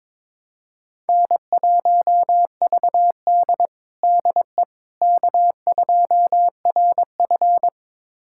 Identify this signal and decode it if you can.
Morse code, heard as a keyed tone, about 22 wpm, 700 Hz. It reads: N1VD DE K2RF
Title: USA Callsign DE Callsign - 4 Characters - 22wpm